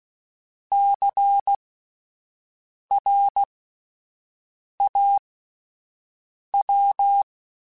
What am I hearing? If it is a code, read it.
Morse code, heard as a keyed tone, about 16 wpm, spaced as for 5 wpm, 800 Hz. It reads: CRAW
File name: Alphabetical / Alphabetical mp3s / Cs @5